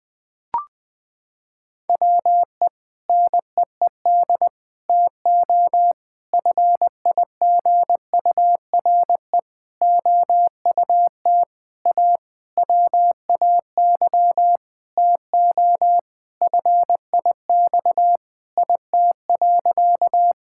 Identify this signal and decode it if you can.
Morse code, heard as a keyed tone, about 20 wpm, 700 Hz. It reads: WE NEED TO FIGURE OUT A WAY TO FIX IT.
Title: Sentences with 6-Word Phrases - 20wpm